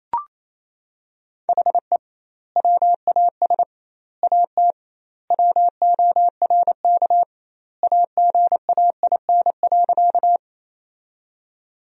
Morse code, heard as a keyed tone, about 28 wpm, 700 Hz. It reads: HE WAS AT WORK AGAIN.
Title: 2x Word-Spacing - Sentences from Top 200 Words - 28wpm